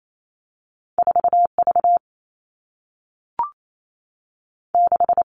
Morse code, heard as a keyed tone, about 28 wpm, 700 Hz. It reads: 4 V 6
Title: Two Challenging Letter-Number Combinations 28wpm